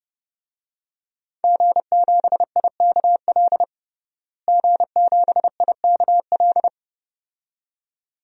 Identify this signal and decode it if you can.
Morse code, heard as a keyed tone, about 30 wpm, 700 Hz. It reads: G7IKL G7IKL
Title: DXCC-100 Call Signs - 5 Characters - Repeated 2x - 30wpm